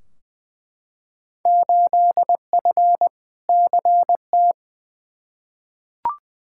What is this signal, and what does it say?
Morse code, heard as a keyed tone, about 20 wpm, 700 Hz. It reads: 8F CT
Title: Field Day Exchanges Encore - 20wpm